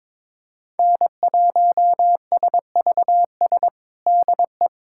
Morse code, heard as a keyed tone, about 21 wpm, 700 Hz. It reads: N1SVS DE
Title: USA Callsign DE Callsign - 5 Characters - 22wpm